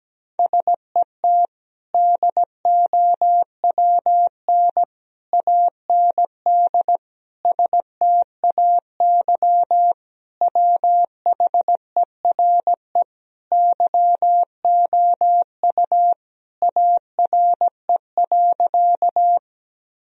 Morse code, heard as a keyed tone, about 17 wpm, 700 Hz. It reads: SET DOWN AND STAY WHERE YOU ARE.
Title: Sentences from Top 1000 Words - Review Part 1 - 17wpm